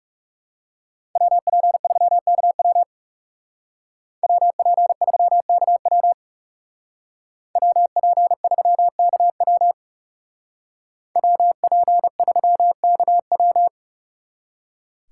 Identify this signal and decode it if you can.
Morse code, heard as a keyed tone, about 39 wpm, 700 Hz. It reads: WP3KW WP3KW WP3KW WP3KW